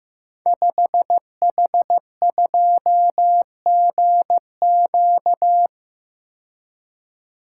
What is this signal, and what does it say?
Morse code, heard as a keyed tone, about 15 wpm, 700 Hz. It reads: 5H2GQ
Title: Code Groups - 5 Letter:Numbers - 15wpm